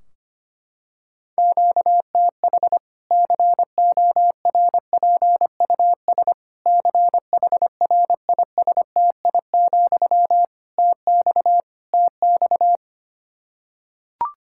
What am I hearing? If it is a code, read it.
Morse code, heard as a keyed tone, about 25 wpm, 700 Hz. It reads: QTH CORPUS CHRISTI, TX TX
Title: Practice QSO - Exchange 1 - Beginning - 25wpm